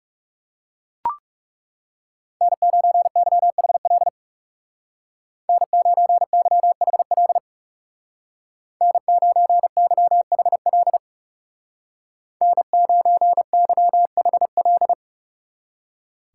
Morse code, heard as a keyed tone, about 36 wpm, 700 Hz. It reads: N9YHL N9YHL N9YHL N9YHL
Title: Speed Racing - USA Call Signs 30wpm